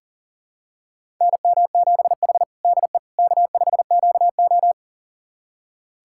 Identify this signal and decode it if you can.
Morse code, heard as a keyed tone, about 40 wpm, 700 Hz. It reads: NM7H DE K5QO
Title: USA Callsign DE Callsign - 4 Characters - 40wpm